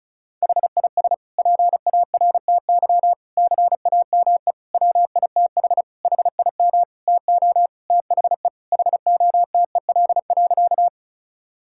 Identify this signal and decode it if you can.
Morse code, heard as a keyed tone, about 34 wpm, 700 Hz. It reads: HIS PARTY CAME WITH HIM TO THE HOTEL.
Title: Sentences from Top 1600 Words 35wpm